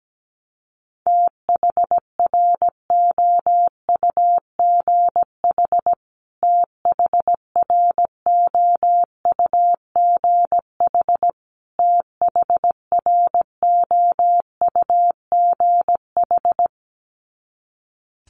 Morse code, heard as a keyed tone, about 17 wpm, 700 Hz. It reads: THROUGH THROUGH THROUGH
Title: Top 500 Words Repeated 17wpm